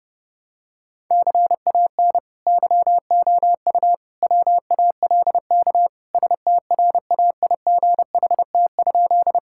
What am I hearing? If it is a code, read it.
Morse code, heard as a keyed tone, about 30 wpm, 700 Hz. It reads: CAN YOU WALK STRAIGHT?
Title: Sentences from Top 1000 Words - Review Part 6 - 30wpm